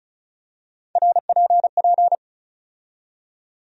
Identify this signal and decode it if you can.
Morse code, heard as a keyed tone, about 34 wpm, 700 Hz. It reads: RPP
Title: USA Call Sign Postfixes - 3 Letters 35wpm